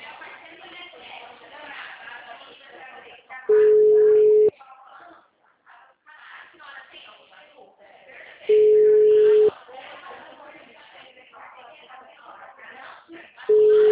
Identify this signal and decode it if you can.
Ringback tone (Europe, 425 Hz), with speech.